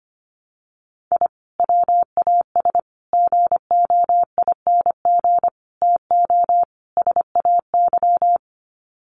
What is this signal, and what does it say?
Morse code, heard as a keyed tone, about 25 wpm, 700 Hz. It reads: I WAS GOING TO SAY
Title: Sets of 5 Words 25wpm